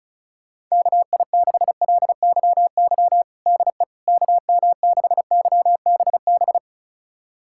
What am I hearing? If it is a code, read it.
Morse code, heard as a keyed tone, about 36 wpm, 700 Hz. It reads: KI6LYY DE KM6YBB
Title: USA Callsign DE Callsign - 6 Characters - 35wpm